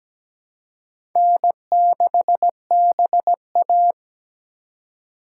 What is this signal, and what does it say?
Morse code, heard as a keyed tone, about 17 wpm, 700 Hz. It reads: N6BA
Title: USA Call Signs - 4 Characters - 2x Character Spacing - 17wpm